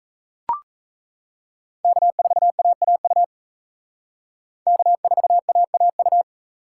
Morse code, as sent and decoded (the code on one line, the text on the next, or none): -.- ....- .- .- ..- / -.- ....- .- .- ..-
K4AAU K4AAU